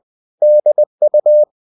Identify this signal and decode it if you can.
Morse code, heard as a keyed tone, about 20 wpm, 600 Hz. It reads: DU
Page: Ferie 20 wpm